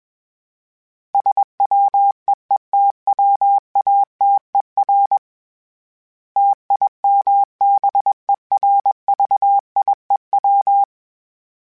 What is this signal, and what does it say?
Morse code, heard as a keyed tone, about 21 wpm, 800 Hz. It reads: SWEETWATER TIMBERVIEW
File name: fromSMs / FromSMs mp3s / Cities@20